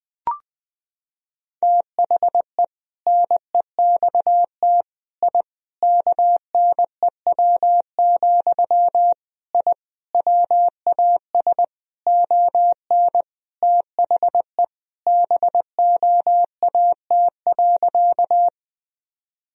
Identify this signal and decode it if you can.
Morse code, heard as a keyed tone, about 21 wpm, 700 Hz. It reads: THE NEXT I KNEW, I WAS ON THE BOAT.
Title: Sentences from Top 1000 Words - Review Part 3 - 20wpm